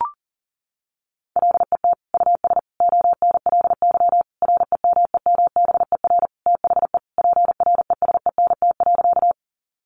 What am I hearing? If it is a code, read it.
Morse code, heard as a keyed tone, about 40 wpm, 700 Hz. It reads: LET US ONLY REMEMBER THE PRESENT.
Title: Sentences from Top 800 Words 40wpm